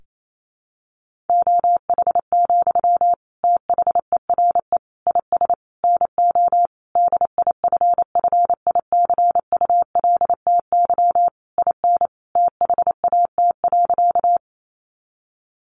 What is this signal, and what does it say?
Morse code, heard as a keyed tone, about 28 wpm, 700 Hz. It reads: OH, THERE IS NO DIFFICULTY IN THAT.
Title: Sentences from Top 1500 Words 28wpm